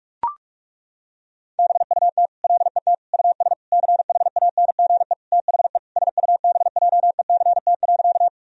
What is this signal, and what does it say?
Morse code, heard as a keyed tone, about 41 wpm, 700 Hz. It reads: BUT LET US CHANGE THE SUBJECT.
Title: Sentences from Top 1000 Words - Review Part 1 - 45wpm